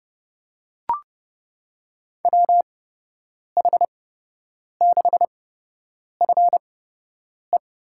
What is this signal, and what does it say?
Morse code, heard as a keyed tone, about 30 wpm, spaced as for 8 wpm, 700 Hz. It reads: WH6FE
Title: USA Call Signs - 6 Characters - 8x Character Spacing - 30wpm